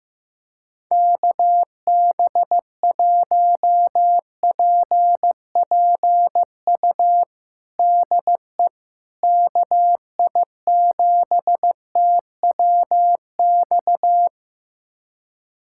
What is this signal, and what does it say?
Morse code, heard as a keyed tone, about 15 wpm, 700 Hz. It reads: KB1PPU DE KI7TWX
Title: USA Callsign DE Callsign - 6 Characters - 15wpm